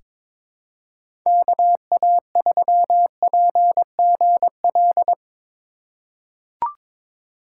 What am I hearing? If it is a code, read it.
Morse code, heard as a keyed tone, about 22 wpm, 700 Hz. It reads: KA3PGL
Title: USA Call Signs - 6 Characters - 2x Character Spacing - 22wpm